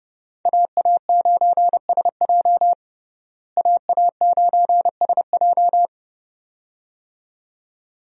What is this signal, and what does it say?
Morse code, heard as a keyed tone, about 30 wpm, 700 Hz. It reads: AA9SJ AA9SJ
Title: USA Call Signs - 5 Characters - Repeated 2x - 30wpm